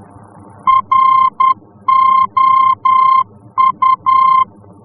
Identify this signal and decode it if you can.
Morse code, heard as a keyed tone, about 10 wpm, 1050 Hz. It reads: ROU
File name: france / mx / ROU-F-VOR